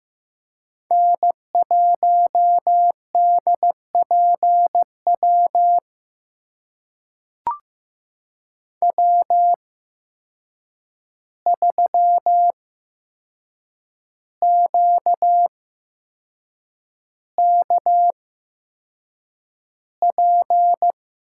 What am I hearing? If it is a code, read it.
Morse code, heard as a keyed tone, about 15 wpm, spaced as for 6 wpm, 700 Hz. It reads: N1DPW W3QKP
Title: USA Call Signs - 5 Characters - 8x Character Spacing - 15wpm